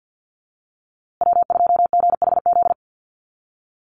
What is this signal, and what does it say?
Morse code, heard as a keyed tone, about 50 wpm, 700 Hz. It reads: W2GHZ
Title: USA Call Signs Repeated 2x 50wpm